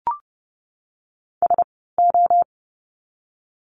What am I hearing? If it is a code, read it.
Morse code, heard as a keyed tone, about 30 wpm, 700 Hz. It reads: S O
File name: Lesson 16f - Letter c - Words 30wpm